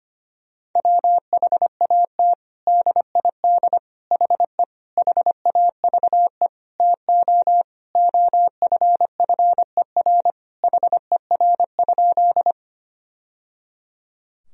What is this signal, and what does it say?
Morse code, heard as a keyed tone, about 25 wpm, 700 Hz. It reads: WHAT DID HE HAVE TO OFFER HER?
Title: Sentences from Top 1000 Words - Review Part 4 - 25wpm